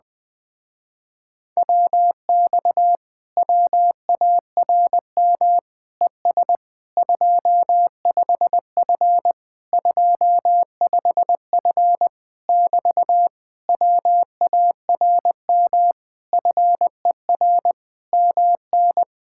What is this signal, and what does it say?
Morse code, heard as a keyed tone, about 20 wpm, 700 Hz. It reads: WX WARM ES 25F 25F <BT> WARM FER MN